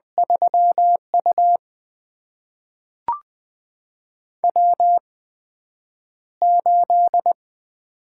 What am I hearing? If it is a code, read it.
Morse code, heard as a keyed tone, about 20 wpm, spaced as for 9 wpm, 700 Hz. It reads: 3U W8
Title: USA Call Signs - 8x Character Spacing - 20wpm